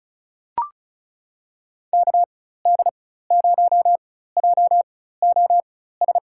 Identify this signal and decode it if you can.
Morse code, heard as a keyed tone, about 36 wpm, spaced as for 16 wpm, 700 Hz. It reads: KD0JOS
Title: USA Call Signs - 6 Characters - 4x Character Spacing - 35wpm